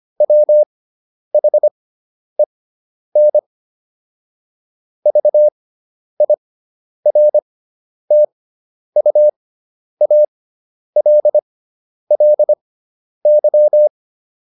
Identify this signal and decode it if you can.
Morse code, heard as a keyed tone, about 25 wpm, spaced as for 10 wpm, 600 Hz. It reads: WHEN VIRTUALLY
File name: SS213 10